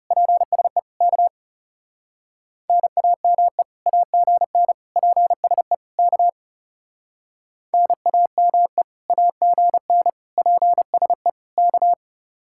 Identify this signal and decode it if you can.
Morse code, heard as a keyed tone, about 33 wpm, 700 Hz. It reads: PSE K NAME AGN PSE K NAME AGN PSE K
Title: Speed-Racing Common QSO Phrases - 30wpm